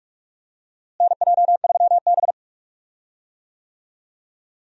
Morse code, heard as a keyed tone, about 46 wpm, 700 Hz. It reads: NJ3B
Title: USA Call Signs - 4 Characters - Repeated 3x - 45wpm